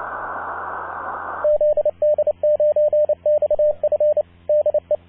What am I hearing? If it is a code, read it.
Morse code, heard as a keyed tone, about 28 wpm, 590 Hz, with radio noise. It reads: ZD9XF DE
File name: zd9xf-on-24mhz-12-sept-2014-13-04utc-cw